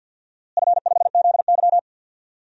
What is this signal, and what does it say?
Morse code, heard as a keyed tone, about 51 wpm, 700 Hz. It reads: F5ZY
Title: DXCC-100 Call Signs - 4 Characters - Repeated 2x - 50wpm